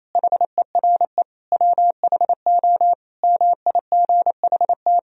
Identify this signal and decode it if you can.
Morse code, heard as a keyed tone, about 28 wpm, 700 Hz. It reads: HERE WHO MIGHT
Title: Sentences from Top 1000 Words - Review Part 3 - 28wpm